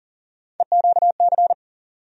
Morse code, heard as a keyed tone, about 40 wpm, 700 Hz. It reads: EQC
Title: USA Call Sign Postfixes - 3 Letters 40wpm